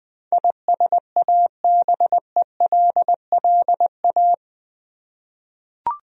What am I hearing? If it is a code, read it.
Morse code, heard as a keyed tone, about 21 wpm, 700 Hz. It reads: ISABELLA
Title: USA Common Names - 20wpm